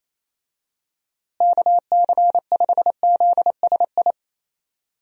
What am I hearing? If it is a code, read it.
Morse code, heard as a keyed tone, about 28 wpm, 700 Hz. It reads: KC5ZSI
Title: USA Call Signs - 6 Characters - Repeated 2x - 28wpm